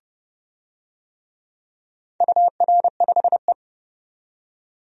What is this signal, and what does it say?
Morse code, heard as a keyed tone, about 31 wpm, 700 Hz. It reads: UR5E
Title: DXCC-100 Call Signs - 4 Characters - 30wpm